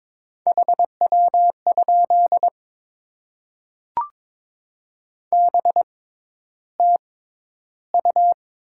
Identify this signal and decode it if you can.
Morse code, heard as a keyed tone, about 22 wpm, spaced as for 11 wpm, 700 Hz. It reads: HW? BTU
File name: Top 20 QSO Elements - 6x Character Spacing - 22wpm